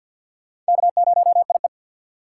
Morse code, heard as a keyed tone, about 48 wpm, 700 Hz. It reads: K0IE
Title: USA Call Signs - 4 Characters - 6x Character Spacing - 50wpm